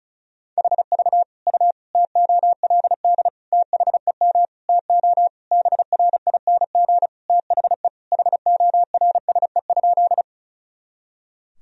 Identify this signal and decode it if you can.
Morse code, heard as a keyed tone, about 36 wpm, 700 Hz. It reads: HV U TOLD THEM TO BRING THE HORSE?
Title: Sentences from Top 500 Words and Abbreviations 35wpm